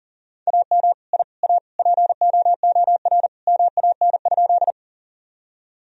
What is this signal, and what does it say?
Morse code, heard as a keyed tone, about 40 wpm, 700 Hz. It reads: AM I A POOR MAN?
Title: Sentences from Top 1000 Words - Review Part 2 - 40wpm